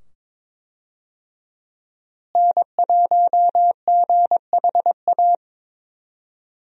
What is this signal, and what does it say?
Morse code, heard as a keyed tone, about 22 wpm, 700 Hz. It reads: N1GHA
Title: USA Call Signs - 5 Characters - Repeated 2x - 22wpm